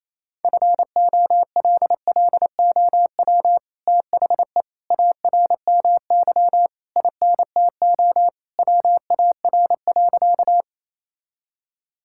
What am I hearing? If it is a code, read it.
Morse code, heard as a keyed tone, about 28 wpm, 700 Hz. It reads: FOLLOW THE ARMY INTO WAR.
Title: Sentences from Top 1000 Words - Review Part 5 - 28wpm